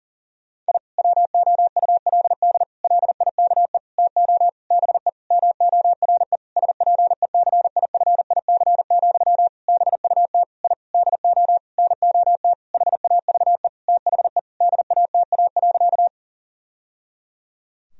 Morse code, heard as a keyed tone, about 40 wpm, 700 Hz. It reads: I WOULD LIKE TO BE MORE SPECIFIC, BUT I DO NOT HAVE THE DATA.
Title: Sentences from Top 1000 Words - Review Part 3 - 40wpm